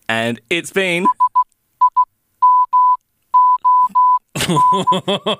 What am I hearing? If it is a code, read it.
Morse code, heard as a keyed tone, about 16 wpm, spaced as for 12 wpm, 1000 Hz, with speech. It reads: SIMON